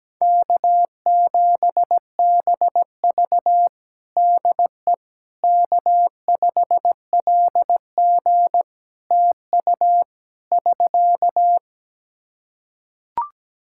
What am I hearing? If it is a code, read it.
Morse code, heard as a keyed tone, about 17 wpm, 700 Hz. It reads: K7BV DE K5LG TU <SK>
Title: K1USN SST QSO Practice 17wpm